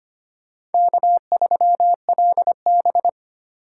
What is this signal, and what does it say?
Morse code, heard as a keyed tone, about 33 wpm, 700 Hz. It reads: K3LB